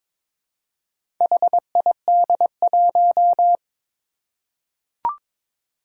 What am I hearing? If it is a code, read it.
Morse code, heard as a keyed tone, about 22 wpm, 700 Hz. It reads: HID1
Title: Four Challenging Letter-Number Combinations 22wpm